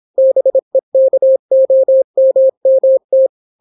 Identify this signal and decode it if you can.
Morse code, heard as a keyed tone, about 26 wpm, 530 Hz. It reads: BEKOMMT